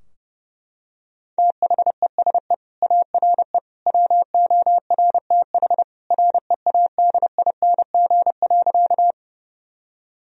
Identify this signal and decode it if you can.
Morse code, heard as a keyed tone, about 30 wpm, 700 Hz. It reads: THESE ARE WORTH READING.
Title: Sentences from Top 1000 Words - Review Part 3 - 30wpm